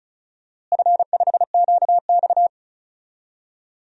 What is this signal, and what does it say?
Morse code, heard as a keyed tone, about 34 wpm, 700 Hz. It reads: F5QX
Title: DXCC-100 Call Signs - 4 Characters - Repeated 2x - 35wpm